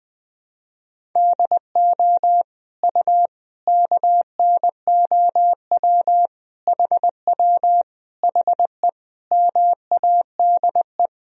Morse code, heard as a keyed tone, about 20 wpm, 700 Hz. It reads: DO U KNOW HW HE MADE
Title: Sentences from Top 300 Words and Abbreviations 20wpm